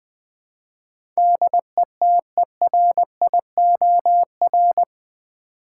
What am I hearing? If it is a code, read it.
Morse code, heard as a keyed tone, about 20 wpm, 700 Hz. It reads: DETERIOR
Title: Build A Word - 4 Letters At A Time - 11-Letter Words - 20wpm